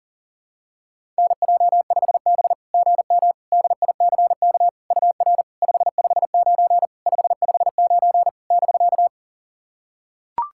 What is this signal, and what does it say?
Morse code, heard as a keyed tone, about 40 wpm, 690 Hz. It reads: NJ5B GM DICK UR 559 559 <BK>
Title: SOTA QSO Practice 40wpm